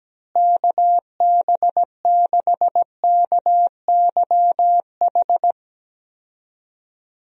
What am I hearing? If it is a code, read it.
Morse code, heard as a keyed tone, about 17 wpm, 700 Hz. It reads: KB6KYH
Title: USA Call Signs - 6 Characters - Repeated 2x - 17wpm